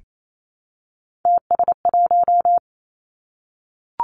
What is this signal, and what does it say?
Morse code, heard as a keyed tone, about 28 wpm, 700 Hz. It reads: TS1
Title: Three Challenging Letter-Number Combinations 28wpm